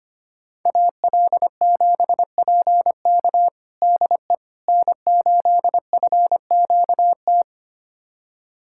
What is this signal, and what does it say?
Morse code, heard as a keyed tone, about 25 wpm, 700 Hz. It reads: AL7PK DE N8FQT
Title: USA Callsign DE Callsign - 5 Characters - 25wpm